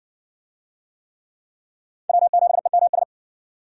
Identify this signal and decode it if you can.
Morse code, heard as a keyed tone, about 63 wpm, 700 Hz. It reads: W7EMS